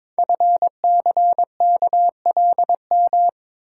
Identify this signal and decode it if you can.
Morse code, heard as a keyed tone, about 22 wpm, 700 Hz. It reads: FCKLM
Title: Code Groups - 5 Letters - 22wpm